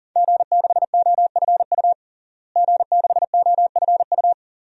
Morse code, heard as a keyed tone, about 40 wpm, 700 Hz. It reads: G6OFU G6OFU
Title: DXCC-100 Call Signs - 5 Characters - Repeated 2x - 40wpm